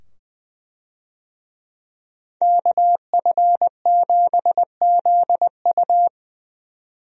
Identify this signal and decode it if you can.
Morse code, heard as a keyed tone, about 21 wpm, 700 Hz. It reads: KF7ZU
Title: USA Call Signs - 5 Characters - 20wpm